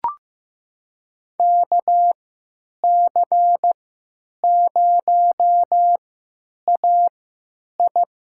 Morse code, heard as a keyed tone, about 15 wpm, spaced as for 9 wpm, 700 Hz. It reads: KC0AI
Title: USA Call Signs - 3x Character Spacing - 15wpm